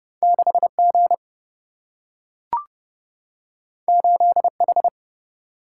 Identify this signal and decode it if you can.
Morse code, heard as a keyed tone, about 30 wpm, 700 Hz. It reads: 6G 8H
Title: Single-Letter Single-Number 30wpm